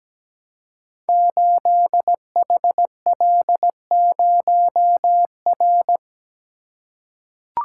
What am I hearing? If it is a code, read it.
Morse code, heard as a keyed tone, about 17 wpm, 700 Hz. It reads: 8HL0R
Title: Code Groups - 5 Letter:Numbers - 17wpm